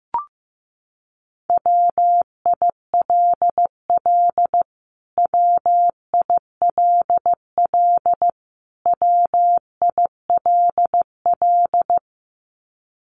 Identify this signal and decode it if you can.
Morse code, heard as a keyed tone, about 15 wpm, 700 Hz. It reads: WILL WILL WILL
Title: Top 500 Words Repeated 15wpm